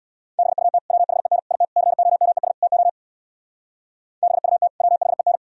Morse code, heard as a keyed tone, about 60 wpm, 690 Hz. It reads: BUT WHEN IT COMES TO BUT WHEN
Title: Speed-Racing - Sets of 5 Words 50wpm